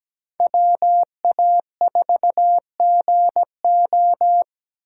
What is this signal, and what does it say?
Morse code, heard as a keyed tone, about 20 wpm, 700 Hz. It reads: WA4GO
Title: Speed Racing - USA 2x2 Call Signs 17wpm